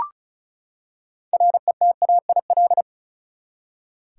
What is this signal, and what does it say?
Morse code, heard as a keyed tone, about 36 wpm, 700 Hz. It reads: RETAIL
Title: Practice QSO - Exchange 3 - More Info - 35wpm